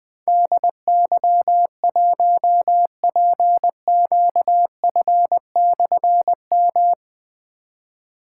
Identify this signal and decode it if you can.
Morse code, heard as a keyed tone, about 20 wpm, 700 Hz. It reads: DY1PQF/M
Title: Difficult Call Signs 20wpm